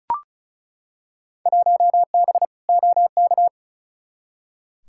Morse code, heard as a keyed tone, about 36 wpm, 700 Hz. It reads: 1B OK
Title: Field Day Exchanges Encore - 35wpm